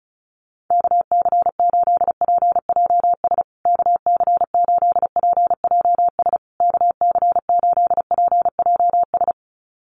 Morse code, heard as a keyed tone, about 34 wpm, 700 Hz. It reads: KC8PJS KC8PJS KC8PJS
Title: USA Call Signs Repeated 3x 35wpm